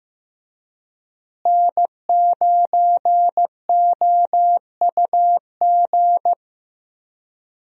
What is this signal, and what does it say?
Morse code, heard as a keyed tone, about 15 wpm, 700 Hz. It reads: N9OUG
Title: USA Call Signs - 5 Characters - Repeated 2x - 15wpm